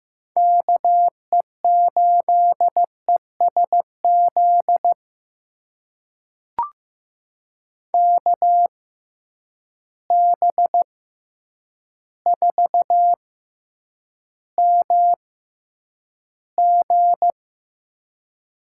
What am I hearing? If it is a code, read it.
Morse code, heard as a keyed tone, about 15 wpm, spaced as for 8 wpm, 700 Hz. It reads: KE8ESZ KB4MG
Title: USA Call Signs - 6 Characters - 6x Character Spacing - 15wpm